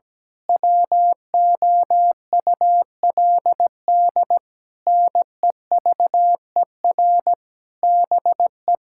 Morse code, heard as a keyed tone, about 17 wpm, 700 Hz. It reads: WOULD NEVER BE
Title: Sentences from Top 1600 Words 17wpm